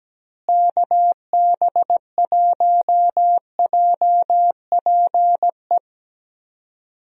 Morse code, heard as a keyed tone, about 17 wpm, 700 Hz. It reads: KB1JPE
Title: USA Call Signs - 6 Characters - Repeated 3x - 17wpm